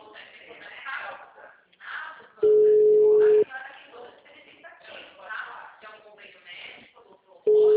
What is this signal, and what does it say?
Ringback tone (Europe, 425 Hz), with speech.
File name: out- -2007- - - .32896